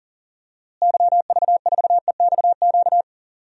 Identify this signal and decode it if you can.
Morse code, heard as a keyed tone, about 40 wpm, 700 Hz. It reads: YV4EXQ
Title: DXCC-100 Call Signs - 6 Characters - Repeated 2x - 40wpm